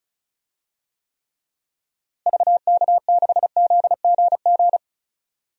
Morse code, heard as a keyed tone, about 36 wpm, 700 Hz. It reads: VK6ZGG
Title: DXCC-100 Call Signs - 6 Characters - Repeated 2x - 35wpm